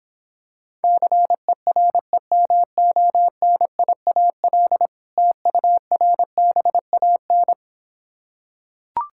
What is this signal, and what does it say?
Morse code, heard as a keyed tone, about 26 wpm, 700 Hz. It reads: CEREMONIAL TURBAN